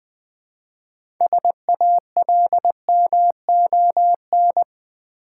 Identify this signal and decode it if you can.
Morse code, heard as a keyed tone, about 20 wpm, 700 Hz. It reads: SALMON
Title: Easy Words with Hints - 20wpm